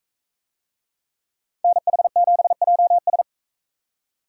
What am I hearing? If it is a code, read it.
Morse code, heard as a keyed tone, about 39 wpm, 700 Hz. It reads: NH7JS